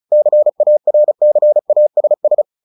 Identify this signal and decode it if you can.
Morse code, heard as a keyed tone, about 34 wpm, 600 Hz. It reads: CARCASS
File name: the old man and the sea 2 35wpm